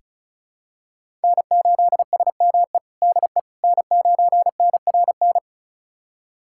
Morse code, heard as a keyed tone, about 34 wpm, 700 Hz. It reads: N8SME DE N9NRN
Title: USA Callsign DE Callsign - 5 Characters - 35wpm